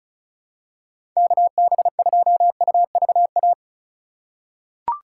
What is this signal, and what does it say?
Morse code, heard as a keyed tone, about 34 wpm, 700 Hz. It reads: KB2UVA
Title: USA Call Signs - 6 Characters - 3x Character Spacing - 35wpm